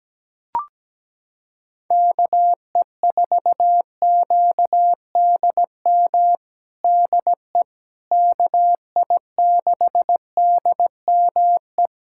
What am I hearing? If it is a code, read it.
Morse code, heard as a keyed tone, about 17 wpm, 700 Hz. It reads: KE4QDM DE KI6DME
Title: USA Callsign DE Callsign - 6 Characters - 17wpm